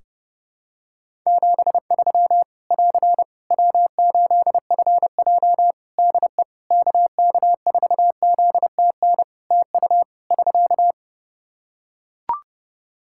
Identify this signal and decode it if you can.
Morse code, heard as a keyed tone, about 30 wpm, 700 Hz. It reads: 73 <AR> W8FJ DE KK4ZTN TU <SK>
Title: Practice QSO - Exchange 4 - Ending - 30wpm